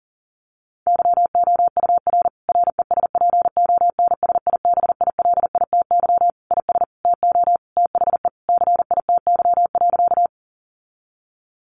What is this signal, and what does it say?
Morse code, heard as a keyed tone, about 40 wpm, 700 Hz. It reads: YOUR RESPONSIBILITY IS TO THE CITY.
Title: Sentences from Top 1200 Words 40wpm